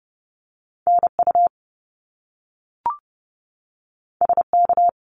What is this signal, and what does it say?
Morse code, heard as a keyed tone, about 30 wpm, 700 Hz. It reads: NU SK
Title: Canadian Provinces and Territories 30wpm